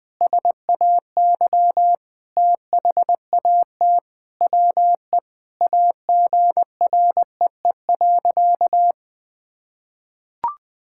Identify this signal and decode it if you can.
Morse code, heard as a keyed tone, about 20 wpm, 700 Hz. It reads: SAY THAT WE AGREE.
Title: Sentences with 8-Word Phrases - 20wpm